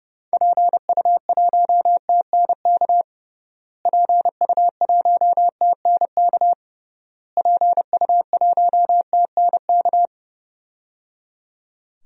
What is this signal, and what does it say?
Morse code, heard as a keyed tone, about 30 wpm, 700 Hz. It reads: PU1TNK PU1TNK PU1TNK
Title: DXCC-100 Call Signs - 6 Characters - Repeated 3x - 30wpm